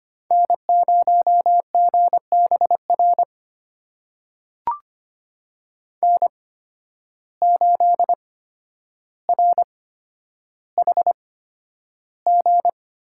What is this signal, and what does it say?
Morse code, heard as a keyed tone, about 25 wpm, spaced as for 11 wpm, 700 Hz. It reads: N0GBR N8RHG
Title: USA Call Signs - 5 Characters - 8x Character Spacing - 25wpm